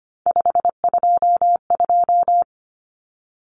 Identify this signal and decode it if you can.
Morse code, heard as a keyed tone, about 25 wpm, 700 Hz. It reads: 522
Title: Common Cut Numbers - 3-Digit 25wpm